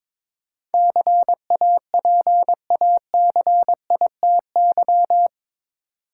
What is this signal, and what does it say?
Morse code, heard as a keyed tone, about 22 wpm, 700 Hz. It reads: CAPACITY
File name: practice / 22wpm / Eight Letter Words - 22wpm